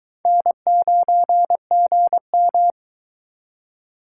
Morse code, heard as a keyed tone, about 23 wpm, 700 Hz. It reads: N9GM